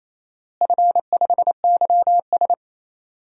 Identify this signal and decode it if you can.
Morse code, heard as a keyed tone, about 28 wpm, 700 Hz. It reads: F5YS
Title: DXCC-100 Call Signs - 4 Characters - Repeated 2x - 28wpm